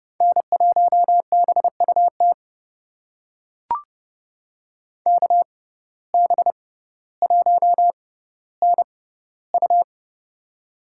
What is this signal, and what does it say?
Morse code, heard as a keyed tone, about 31 wpm, spaced as for 16 wpm, 700 Hz. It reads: N1BUT KB1NU
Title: USA Call Signs - 5 Characters - 6x Character Spacing - 30wpm